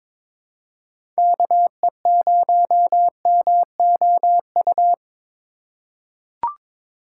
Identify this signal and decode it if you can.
Morse code, heard as a keyed tone, about 22 wpm, 700 Hz. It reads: KE0MOU
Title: USA Call Signs - 3x Character Spacing - 22wpm